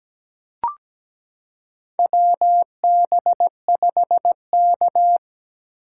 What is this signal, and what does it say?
Morse code, heard as a keyed tone, about 17 wpm, 700 Hz. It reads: WB5K
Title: K1USN SST QSO Practice - 17wpm